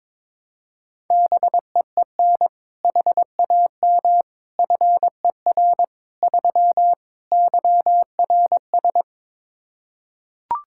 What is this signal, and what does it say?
Morse code, heard as a keyed tone, about 23 wpm, 700 Hz. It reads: BEEN HAM FER 3 YRS
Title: K1USN SST QSO Practice 22wpm